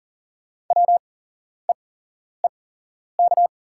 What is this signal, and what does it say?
Morse code, heard as a keyed tone, about 40 wpm, spaced as for 11 wpm, 690 Hz. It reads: WEEK
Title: 8x Character Spacing - Top 500 Words - 40wpm